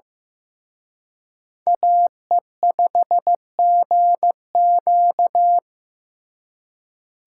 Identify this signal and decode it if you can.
Morse code, heard as a keyed tone, about 18 wpm, 700 Hz. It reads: AE5GQ
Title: Speed Racing - USA 2x2 Call Signs 15wpm